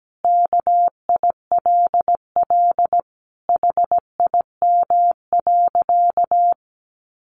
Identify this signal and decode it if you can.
Morse code, heard as a keyed tone, about 17 wpm, 700 Hz. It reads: KILL HIM.
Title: Sentences from Top 1400 Words 17wpm